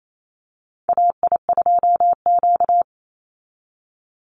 Morse code, heard as a keyed tone, about 28 wpm, 700 Hz. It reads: AI2Q
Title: Call Signs Repeated 2x - 28wpm